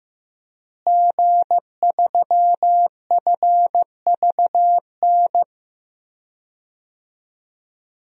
Morse code, heard as a keyed tone, about 15 wpm, 700 Hz. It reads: G3FVN
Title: DXCC-100 Call Signs - 5 Characters - Repeated 3x - 15wpm